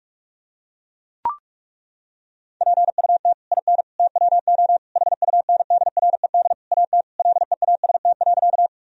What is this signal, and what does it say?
Morse code, heard as a keyed tone, about 41 wpm, 700 Hz. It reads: PUT IN TWO HUNDRED AT LEAST.
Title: Sentences from Top 1000 Words - Review Part 2 - 45wpm